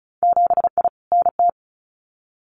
Morse code, heard as a keyed tone, about 36 wpm, 700 Hz. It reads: 7I NT
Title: Winter Field Day Exchanges Encore 35wpm